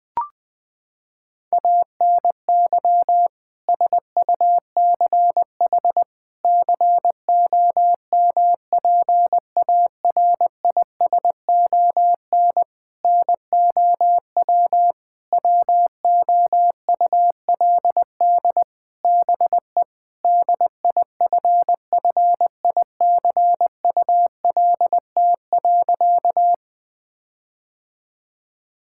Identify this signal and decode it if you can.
Morse code, heard as a keyed tone, about 20 wpm, 700 Hz. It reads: ANY SUCH COMPARISON NOW WOULD BE DIFFICULT.
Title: Sentences from Top 1600 Words 20wpm